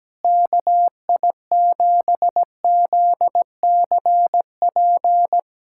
Morse code, heard as a keyed tone, about 17 wpm, 700 Hz. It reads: KI7ZCP
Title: USA Callsign DE Callsign - 6 Characters - 17wpm